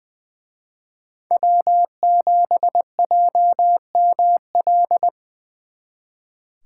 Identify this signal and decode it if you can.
Morse code, heard as a keyed tone, about 24 wpm, 700 Hz. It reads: W7JML
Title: Speed Racing - USA 1x3 Call Signs 20wpm